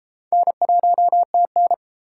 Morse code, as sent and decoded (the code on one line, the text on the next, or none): -. .---- - -.
N1TN